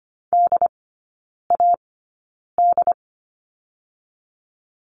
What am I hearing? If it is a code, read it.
Morse code, heard as a keyed tone, about 25 wpm, spaced as for 9 wpm, 690 Hz. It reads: DAD
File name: Lesson 14e - Number 2 - Three Characters 25wpm